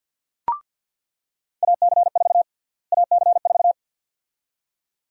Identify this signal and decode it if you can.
Morse code, heard as a keyed tone, about 47 wpm, 700 Hz. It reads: AC4 AC4
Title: USA Call Sign Prefixes - Repeated 2x - 50wpm